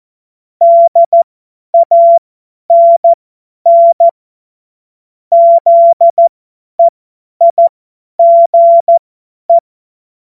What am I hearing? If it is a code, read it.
Morse code, heard as a keyed tone, about 14 wpm, spaced as for 10 wpm, 680 Hz. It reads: DANN ZEIGE
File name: Morsekurs / Witze / Witz 23 Januar 2024 50 F70